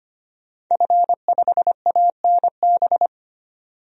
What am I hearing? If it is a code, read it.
Morse code, heard as a keyed tone, about 25 wpm, 700 Hz. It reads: F5ANB
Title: DXCC-100 Call Signs - 5 Characters - 25wpm